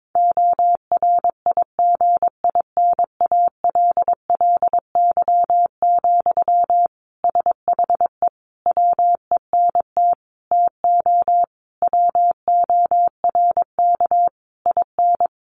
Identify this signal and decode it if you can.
Morse code, heard as a keyed tone, about 22 wpm, 700 Hz. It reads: ORIGINALLY, SHE WENT TO WORK IN
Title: Sentences from Top 1500 Words 22wpm